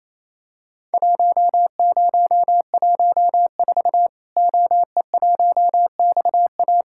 Morse code, heard as a keyed tone, about 28 wpm, 700 Hz. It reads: 1014 OE1XA
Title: Worked All Europe DX Contest - QTC Exchange - 28wpm